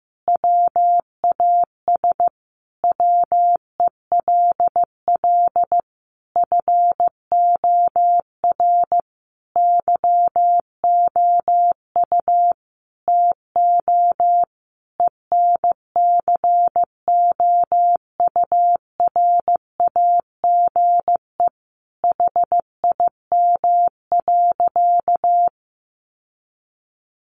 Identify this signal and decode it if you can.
Morse code, heard as a keyed tone, about 15 wpm, 700 Hz. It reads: WAS WELL FOR YOU TO ENCOURAGE HIM.
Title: Sentences from Top 1400 Words 15wpm